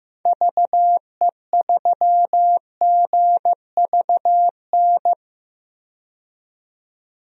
Morse code, heard as a keyed tone, about 15 wpm, 700 Hz. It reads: VE3GVN
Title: DXCC-100 Call Signs - 6 Characters - 15wpm